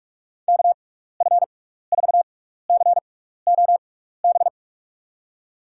Morse code, heard as a keyed tone, about 46 wpm, spaced as for 15 wpm, 700 Hz. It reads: KF4COB
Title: USA Call Signs - 6 Characters - 6x Character Spacing - 45wpm